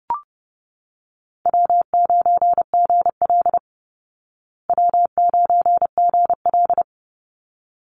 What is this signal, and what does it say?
Morse code, heard as a keyed tone, about 30 wpm, 700 Hz. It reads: W9GL W9GL
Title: Call Signs Repeated 3x - 30wpm